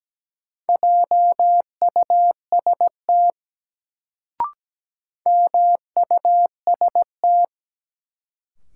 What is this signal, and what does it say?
Morse code, heard as a keyed tone, about 17 wpm, 700 Hz. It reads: JUST MUST
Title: Sentences from Top 200 Words - New Word Drill - 17wpm